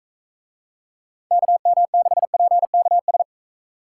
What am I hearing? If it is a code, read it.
Morse code, heard as a keyed tone, about 42 wpm, 700 Hz. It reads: KM6PKS